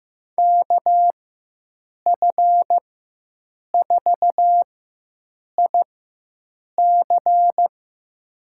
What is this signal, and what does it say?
Morse code, heard as a keyed tone, about 16 wpm, spaced as for 7 wpm, 700 Hz. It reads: KF4IC
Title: USA Call Signs - 5 Characters - 4x Character Spacing - 15wpm